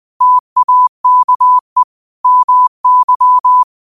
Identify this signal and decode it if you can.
Morse code, heard as a keyed tone, about 20 wpm, 1000 Hz. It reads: TAKE MY
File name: qotdfiles / 20wpm